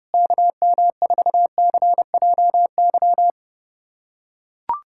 Morse code, heard as a keyed tone, about 32 wpm, 700 Hz. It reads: KM4CJY
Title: USA Callsign DE Callsign - 6 Characters - 30wpm